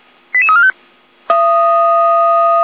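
Five-tone selective call (ZVEI), heard as quick runs of five tones, tones of about 70 ms. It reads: 88345
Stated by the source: ZVEI five-tone call 88288 received on SDR.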